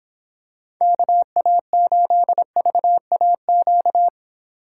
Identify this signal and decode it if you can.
Morse code, heard as a keyed tone, about 26 wpm, 700 Hz. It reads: KA8VAQ